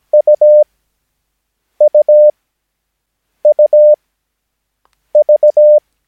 Morse code, heard as a keyed tone, about 17 wpm, spaced as for 6 wpm, 600 Hz. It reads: UUUV
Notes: Övningsmaterial med alfabetet A till Ö samt siffror 0 till 9 enligt International/Continental Morse Code.